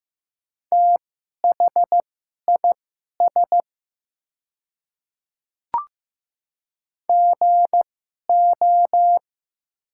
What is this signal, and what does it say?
Morse code, heard as a keyed tone, about 15 wpm, spaced as for 11 wpm, 700 Hz. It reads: THIS GO
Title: 2x Character Spacing - Top 100 Words - 15wpm